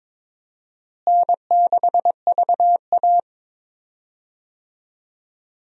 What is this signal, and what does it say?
Morse code, heard as a keyed tone, about 21 wpm, 700 Hz. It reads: N6VA
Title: USA Call Signs - 4 Characters - Repeated 3x - 22wpm